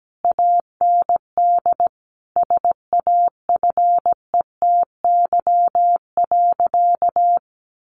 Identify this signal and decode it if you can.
Morse code, heard as a keyed tone, about 17 wpm, 700 Hz. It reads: AND SAFETY.
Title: A Sound Of Thunder 17wpm